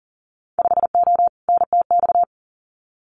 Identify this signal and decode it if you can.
Morse code, heard as a keyed tone, about 38 wpm, 700 Hz. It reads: 5O NTX
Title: Winter Field Day Exchanges Encore 40wpm